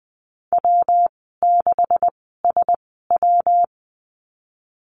Morse code, heard as a keyed tone, about 20 wpm, spaced as for 15 wpm, 700 Hz. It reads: W6SW
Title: 140 Lesson 23g - Letter f - Call Signs 20wpm